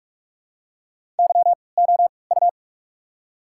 Morse code, heard as a keyed tone, about 45 wpm, spaced as for 25 wpm, 700 Hz. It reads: YOU
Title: 3x Character Spacing - Top 200 Words - 45wpm